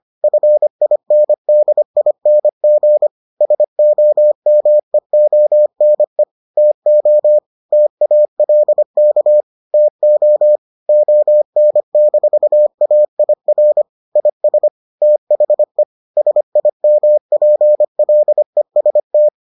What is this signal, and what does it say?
Morse code, heard as a keyed tone, about 25 wpm, 600 Hz. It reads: FINDING SOMEONE TO TALK TO ON-AIR IS THE SIMPLEST